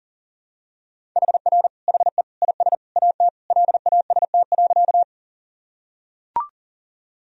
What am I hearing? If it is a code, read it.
Morse code, heard as a keyed tone, about 40 wpm, 700 Hz. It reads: HR HE IS AT LAST.
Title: Sentences from Top 300 Words and Abbreviations 40wpm